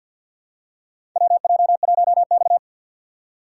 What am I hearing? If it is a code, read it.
Morse code, heard as a keyed tone, about 51 wpm, 700 Hz. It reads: WJ1X
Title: USA Call Signs - 6x Character Spacing - 50wpm